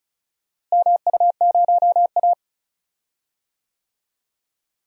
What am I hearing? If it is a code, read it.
Morse code, heard as a keyed tone, about 36 wpm, 700 Hz. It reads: MU0A
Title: DXCC-100 Call Signs - 4 Characters - Repeated 2x - 35wpm